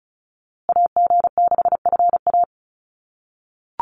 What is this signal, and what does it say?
Morse code, heard as a keyed tone, about 34 wpm, 700 Hz. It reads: AG6FA
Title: USA Call Signs Repeated 3x 35wpm